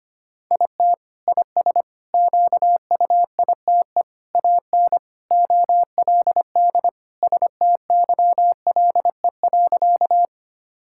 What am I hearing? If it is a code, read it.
Morse code, heard as a keyed tone, about 25 wpm, 700 Hz. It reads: IT IS QUITE AN OLD STYLE.
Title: Sentences from Top 1000 Words - Review Part 5 - 25wpm